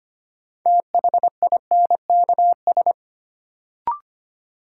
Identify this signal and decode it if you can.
Morse code, heard as a keyed tone, about 25 wpm, 700 Hz. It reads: THINKS
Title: Sentences from Top 1000 Words - Review Part 6 - New Word Drill - 25wpm